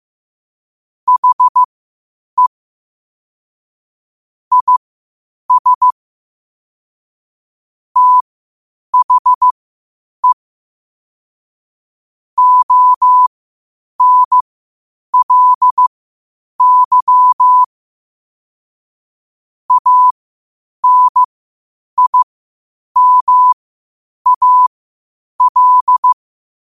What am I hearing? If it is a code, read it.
Morse code, heard as a keyed tone, about 15 wpm, spaced as for 8 wpm, 1000 Hz. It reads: HE IS THE ONLY ANIMAL
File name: qotdfiles / 8wpm